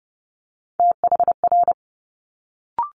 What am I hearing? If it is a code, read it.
Morse code, heard as a keyed tone, about 30 wpm, 700 Hz. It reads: THR
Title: Multi-letter Phonemes Quick Fire 30wpm